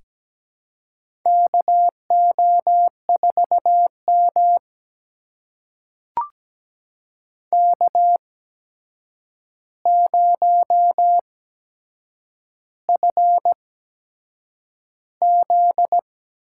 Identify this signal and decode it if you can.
Morse code, heard as a keyed tone, about 17 wpm, spaced as for 7 wpm, 700 Hz. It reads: KO4M K0FZ
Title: USA Call Signs - 4 Characters - 8x Character Spacing - 17wpm